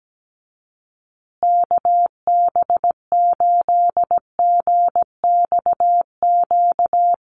Morse code, heard as a keyed tone, about 17 wpm, 700 Hz. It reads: KB8GXQ
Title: USA Call Signs Repeated 2x 17wpm